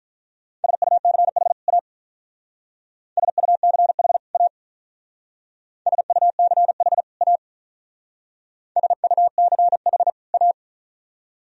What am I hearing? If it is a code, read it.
Morse code, heard as a keyed tone, about 44 wpm, 700 Hz. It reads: SUCH A SUCH A SUCH A SUCH A